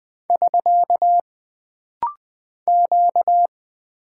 Morse code, heard as a keyed tone, about 20 wpm, 700 Hz. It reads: <SK> Q
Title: Single Character (Letter:Number:Punctuation:Prosign) - Rapid-Fire - 20wpm